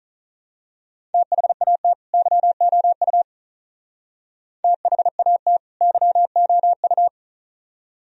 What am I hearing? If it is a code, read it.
Morse code, heard as a keyed tone, about 36 wpm, 700 Hz. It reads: THAT YOU THAT YOU
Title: Speed-Racing - Sets of 2 Words 35wpm